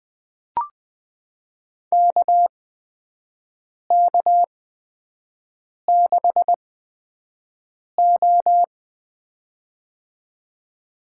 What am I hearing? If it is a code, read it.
Morse code, heard as a keyed tone, about 20 wpm, spaced as for 5 wpm, 700 Hz. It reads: KK6O
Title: USA Call Signs - 4 Characters - 8x Character Spacing - 20wpm